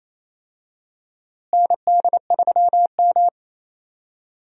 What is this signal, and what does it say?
Morse code, heard as a keyed tone, about 28 wpm, 700 Hz. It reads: ND3M
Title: USA Call Signs - 4 Characters - Repeated 3x - 28wpm